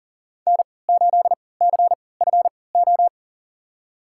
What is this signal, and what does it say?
Morse code, heard as a keyed tone, about 39 wpm, spaced as for 22 wpm, 700 Hz. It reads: N8CFO
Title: USA Call Signs - 5 Characters - 3x Character Spacing - 40wpm